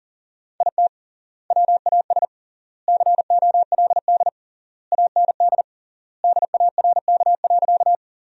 Morse code, heard as a keyed tone, about 40 wpm, 700 Hz. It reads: IT WAS COLD AND DARK.
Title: 3x Word-Spacing - Sentences from Top 500 Words - 40wpm